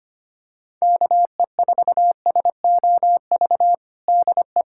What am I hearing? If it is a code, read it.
Morse code, heard as a keyed tone, about 25 wpm, 700 Hz. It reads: KE4SOV DE
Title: USA Callsign DE Callsign - 6 Characters - 25wpm